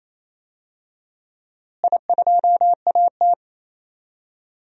Morse code, heard as a keyed tone, about 28 wpm, 700 Hz. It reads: I2AT
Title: DXCC-100 Call Signs - 4 Characters - Repeated 2x - 28wpm